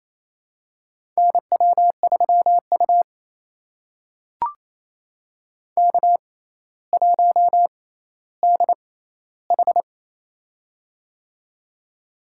Morse code, heard as a keyed tone, about 28 wpm, spaced as for 14 wpm, 700 Hz. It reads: NW3U K1DH
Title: USA Call Signs - 4 Characters - 6x Character Spacing - 28wpm